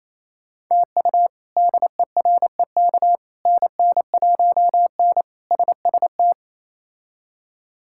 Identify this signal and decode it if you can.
Morse code, heard as a keyed tone, about 28 wpm, 700 Hz. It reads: TU DEREK NN1N SST
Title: K1USN SST QSO Practice - 28wpm